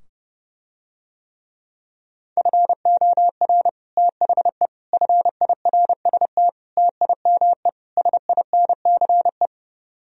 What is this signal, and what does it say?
Morse code, heard as a keyed tone, about 30 wpm, 700 Hz. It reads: FOR THE FIRST TIME SINCE